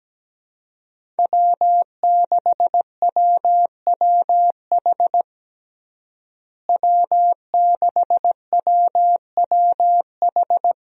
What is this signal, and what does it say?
Morse code, heard as a keyed tone, about 17 wpm, 700 Hz. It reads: W6WWH W6WWH
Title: USA Call Signs - 5 Characters - Repeated 3x - 17wpm